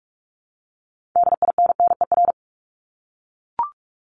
Morse code, heard as a keyed tone, about 44 wpm, 700 Hz. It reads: DINNER
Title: Top 1751-2000 Words 45wpm